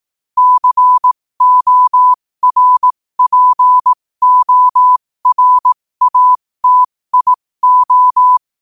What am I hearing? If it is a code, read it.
Morse code, heard as a keyed tone, about 18 wpm, spaced as for 15 wpm, 1000 Hz. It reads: CORPORATIO
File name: qotdfiles / 15wpm